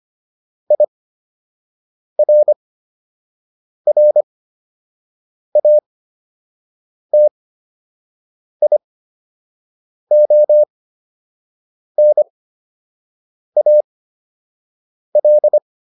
Morse code, heard as a keyed tone, about 25 wpm, spaced as for 6 wpm, 600 Hz. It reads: IRRATIONAL